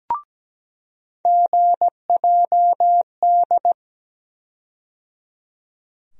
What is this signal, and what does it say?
Morse code, heard as a keyed tone, about 17 wpm, 700 Hz. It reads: GJD
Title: USA Call Sign Postfixes - 3 Letters 17wpm